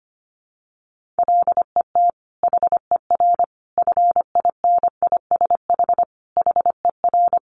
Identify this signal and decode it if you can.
Morse code, heard as a keyed tone, about 25 wpm, 700 Hz. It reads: LET HER FINISH HER
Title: Sentences from Top 1200 Words 25wpm